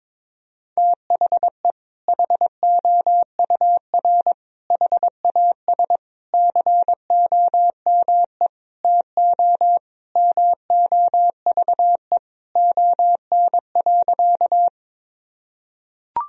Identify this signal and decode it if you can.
Morse code, heard as a keyed tone, about 21 wpm, 700 Hz. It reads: THE HOUR HAS COME TO MOVE ON.
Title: Sentences from Top 1000 Words - Review Part 1 - 22wpm